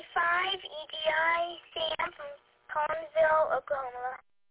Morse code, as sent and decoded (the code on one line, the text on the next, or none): none